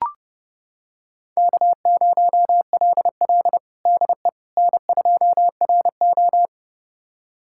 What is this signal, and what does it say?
Morse code, heard as a keyed tone, about 32 wpm, 700 Hz. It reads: K0LL DE N2RO
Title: USA Callsign DE Callsign - 4 Characters - 30wpm